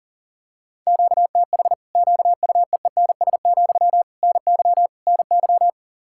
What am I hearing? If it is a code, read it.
Morse code, heard as a keyed tone, about 40 wpm, 700 Hz. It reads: QTH QUEENS, NY NY
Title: Practice QSO - Exchange 1 - Beginning - 40wpm